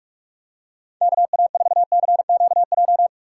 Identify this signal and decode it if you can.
Morse code, heard as a keyed tone, about 47 wpm, 700 Hz. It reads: KA4CQJ